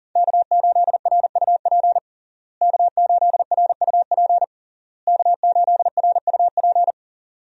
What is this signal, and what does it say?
Morse code, heard as a keyed tone, about 40 wpm, 700 Hz. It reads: K8RUP K8RUP K8RUP
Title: USA Call Signs - 5 Characters - Repeated 3x - 40wpm